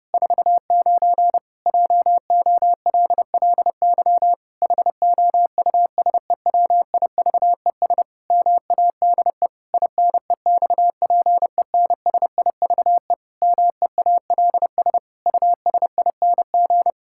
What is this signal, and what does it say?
Morse code, heard as a keyed tone, about 32 wpm, 700 Hz. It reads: 49 JOLLY HOUSEWIVES MADE INEXPENSIVE MEALS USING
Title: Pangram Challenge v2 - 30wpm